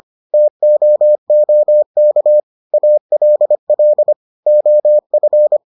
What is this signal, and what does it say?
Morse code, heard as a keyed tone, about 25 wpm, 600 Hz. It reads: TOOK ALL OF